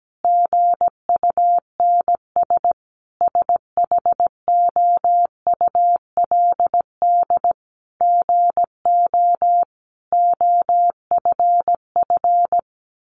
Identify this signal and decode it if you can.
Morse code, heard as a keyed tone, about 17 wpm, 700 Hz. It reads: GUNS SHOULD GO OFF
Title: A Sound Of Thunder 17wpm